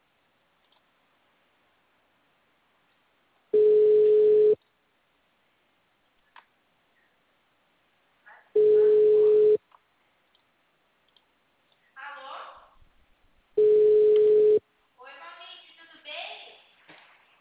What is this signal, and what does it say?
Ringback tone (Europe, 425 Hz), with speech.
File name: out- -2003- - - .77291